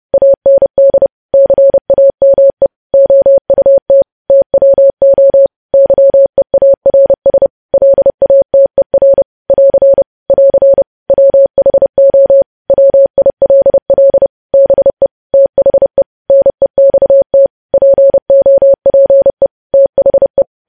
Morse code, heard as a keyed tone, about 30 wpm, 550 Hz. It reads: AND CAME OUT TWO YEARS LATER <AR> <AR> WHO WILL BE THE NEXT POPE THE
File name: podcast / News headlines-1x-30x30